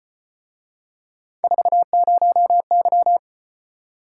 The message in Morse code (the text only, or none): 40Y